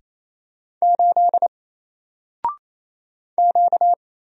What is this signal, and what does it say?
Morse code, heard as a keyed tone, about 27 wpm, 700 Hz. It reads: 8 Q
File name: Single-Letter-Number-28wpm